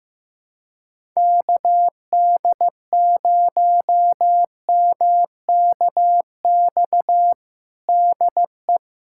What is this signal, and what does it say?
Morse code, heard as a keyed tone, about 15 wpm, 700 Hz. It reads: KD0MKX DE
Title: USA Callsign DE Callsign - 6 Characters - 15wpm